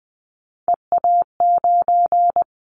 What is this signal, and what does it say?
Morse code, heard as a keyed tone, about 19 wpm, 700 Hz. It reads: EA9
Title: Callsign Prefixes for Top 100 DXCC - 20wpm